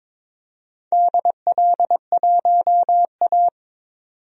Morse code, heard as a keyed tone, about 22 wpm, 700 Hz. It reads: DL1A
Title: DXCC-100 Call Signs - 4 Characters - 22wpm